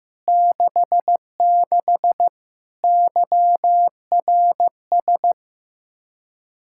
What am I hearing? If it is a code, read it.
Morse code, heard as a keyed tone, about 15 wpm, 700 Hz. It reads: 66 YRS
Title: Practice QSO - Exchange 3 - More Info - 15wpm